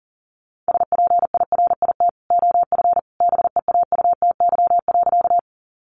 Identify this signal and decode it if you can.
Morse code, heard as a keyed tone, about 40 wpm, 700 Hz. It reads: SPIRIT OF BEAUTY.
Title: Sentences from Top 1400 Words 40wpm